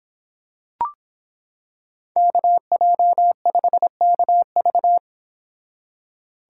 Morse code, heard as a keyed tone, about 26 wpm, 700 Hz. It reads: KJ5KV